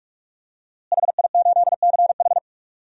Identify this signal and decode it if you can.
Morse code, heard as a keyed tone, about 46 wpm, 700 Hz. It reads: HI8CH
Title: DXCC-100 Call Signs - 5 Characters - Repeated 3x - 45wpm